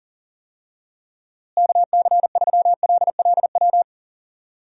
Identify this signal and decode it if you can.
Morse code, heard as a keyed tone, about 37 wpm, 700 Hz. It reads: KC3LLW
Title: USA Call Signs - 6 Characters - Repeated 3x - 40wpm